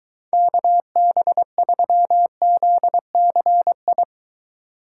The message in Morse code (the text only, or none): KB3ZCI